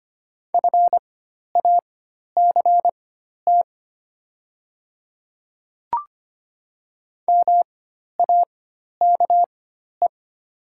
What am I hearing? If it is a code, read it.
Morse code, heard as a keyed tone, about 25 wpm, spaced as for 12 wpm, 700 Hz. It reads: FACT MAKE
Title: 4x Character Spacing - Top 500 Words - 25wpm